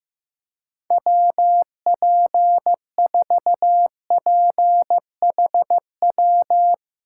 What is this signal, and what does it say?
Morse code, heard as a keyed tone, about 15 wpm, 700 Hz. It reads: WP4PHW
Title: USA Call Signs - 6 Characters - 2x Character Spacing - 15wpm